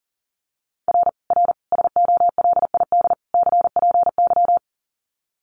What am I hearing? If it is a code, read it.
Morse code, heard as a keyed tone, about 40 wpm, 700 Hz. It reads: R R SOLID CPY
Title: Common QSO Phrases 40wpm